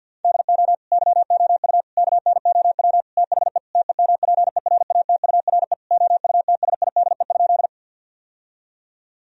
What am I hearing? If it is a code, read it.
Morse code, heard as a keyed tone, about 50 wpm, 700 Hz. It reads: DO YOU KNOW THE TEMPERATURE OUTSIDE?
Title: Sentences from Top 1000 Words - Review Part 4 - 50wpm